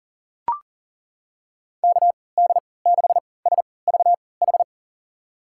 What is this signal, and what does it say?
Morse code, heard as a keyed tone, about 40 wpm, spaced as for 23 wpm, 700 Hz. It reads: KD6SVH
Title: USA Call Signs - 3x Character Spacing - 40wpm